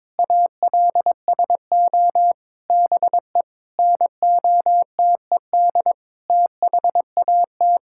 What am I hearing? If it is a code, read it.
Morse code, heard as a keyed tone, about 22 wpm, 700 Hz. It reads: ALSO BE NOTED THAT
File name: practice / 22wpm / Sets of 7 Words - 22wpm